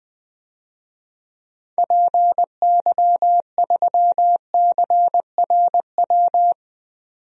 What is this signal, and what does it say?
Morse code, heard as a keyed tone, about 20 wpm, 700 Hz. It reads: PY3CRW
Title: DXCC-100 Call Signs - 6 Characters - 20wpm